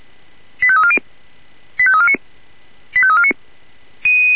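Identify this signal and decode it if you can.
Five-tone selective call (ZVEI), heard as quick runs of five tones, tones of about 70 ms. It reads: 85340 85349 85349
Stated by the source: ZVEI five-tone call 85349 received by SDR.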